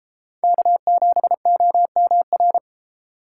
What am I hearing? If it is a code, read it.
Morse code, heard as a keyed tone, about 32 wpm, 700 Hz. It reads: K7OMR